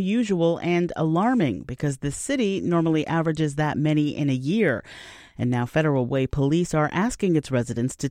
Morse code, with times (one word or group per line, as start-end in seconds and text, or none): none